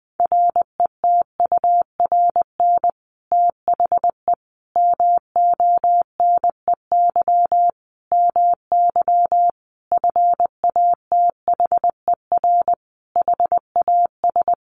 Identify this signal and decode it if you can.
Morse code, heard as a keyed tone, about 20 wpm, 700 Hz. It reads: RETURN THE MONEY MY FATHER HAS
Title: Sentences from Top 1200 Words 20wpm